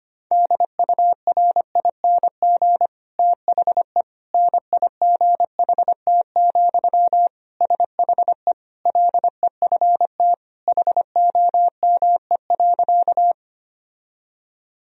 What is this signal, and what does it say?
Morse code, heard as a keyed tone, about 25 wpm, 700 Hz. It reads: DURING THE NIGHT, SHE LEFT HOME.
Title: Sentences from Top 1000 Words - Review Part 2 - 25wpm